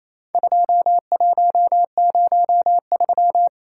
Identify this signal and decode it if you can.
Morse code, heard as a keyed tone, about 28 wpm, 700 Hz. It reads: 2103
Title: Worked All Europe DX Contest - QTC Exchange - 28wpm